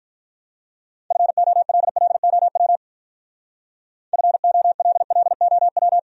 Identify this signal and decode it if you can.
Morse code, heard as a keyed tone, about 45 wpm, 700 Hz. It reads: FOLLOW FOLLOW